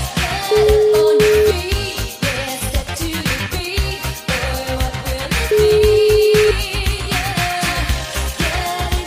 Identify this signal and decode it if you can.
Ringback tone (Europe, 425 Hz), with noise.